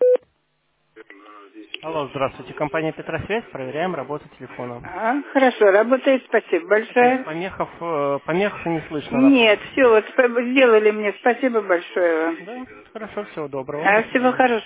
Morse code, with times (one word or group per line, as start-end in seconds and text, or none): none